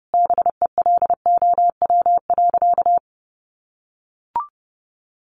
Morse code, heard as a keyed tone, about 30 wpm, 700 Hz. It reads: BELOW.
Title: Sentences from Top 1500 Words 30wpm